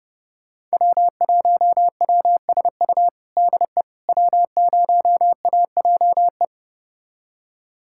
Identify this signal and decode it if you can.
Morse code, heard as a keyed tone, about 31 wpm, 700 Hz. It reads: W1WSU DE W0AJE
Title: USA Callsign DE Callsign - 5 Characters - 30wpm